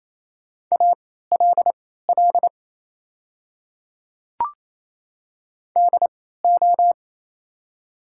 Morse code, heard as a keyed tone, about 28 wpm, spaced as for 16 wpm, 700 Hz. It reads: ALL DO
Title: 3x Character Spacing - Top 100 Words - 28wpm